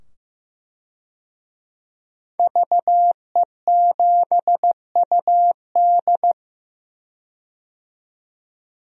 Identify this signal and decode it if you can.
Morse code, heard as a keyed tone, about 15 wpm, 700 Hz. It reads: VE7UD
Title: DXCC-100 Call Signs - 5 Characters - 15wpm